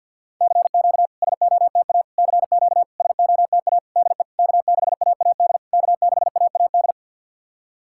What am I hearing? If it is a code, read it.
Morse code, heard as a keyed tone, about 50 wpm, 690 Hz. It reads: CQ SOTA CQ SOTA DE K6AAD K6AAD
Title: SOTA QSO Practice 50wpm